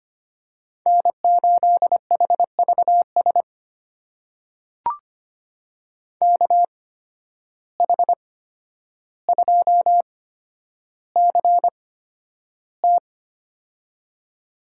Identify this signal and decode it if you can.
Morse code, heard as a keyed tone, about 25 wpm, spaced as for 11 wpm, 700 Hz. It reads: N8HVS KH2CT
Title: USA Call Signs - 5 Characters - 8x Character Spacing - 25wpm